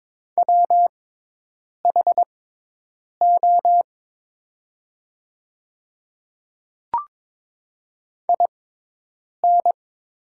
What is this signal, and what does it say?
Morse code, heard as a keyed tone, about 22 wpm, spaced as for 8 wpm, 700 Hz. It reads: WHO IN
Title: 6x Character Spacing - Top 100 Words - 22wpm